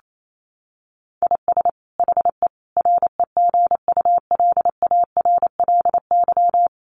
Morse code, heard as a keyed tone, about 28 wpm, 700 Hz. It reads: IS HE REGULARLY
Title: Sentences from Top 1400 Words 28wpm